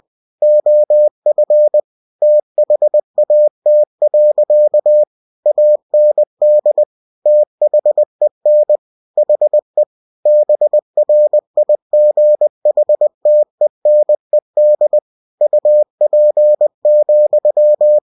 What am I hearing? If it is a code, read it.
Morse code, heard as a keyed tone, about 20 wpm, 600 Hz. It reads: OF THAT. AND THEN HE BRIGHTENED UP,
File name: cw-books / winnie-the-pooh / winnie-20wpm-page-0035